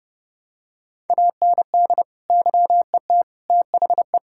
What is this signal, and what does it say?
Morse code, heard as a keyed tone, about 30 wpm, 700 Hz. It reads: AND YET THE
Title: Sentences from Top 1000 Words - Review Part 6 - 30wpm